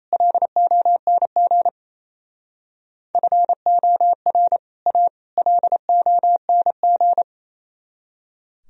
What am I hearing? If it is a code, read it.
Morse code, heard as a keyed tone, about 35 wpm, 700 Hz. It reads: LONG FOR A LONG